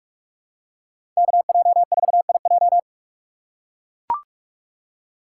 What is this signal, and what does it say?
Morse code, heard as a keyed tone, about 41 wpm, 700 Hz. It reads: KJ4IJ
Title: USA Call Signs - 5 Characters - 2x Character Spacing - 45wpm